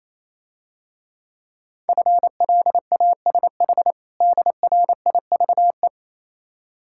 Morse code, heard as a keyed tone, about 27 wpm, 690 Hz. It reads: FLASH DRIVE
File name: Easy Words - Speed-Racing with Hints - 22wpm